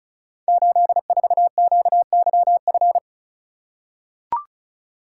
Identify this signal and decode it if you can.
Morse code, heard as a keyed tone, about 36 wpm, 700 Hz. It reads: 84QYF
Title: Code Groups - 5 Letter:Numbers - 35wpm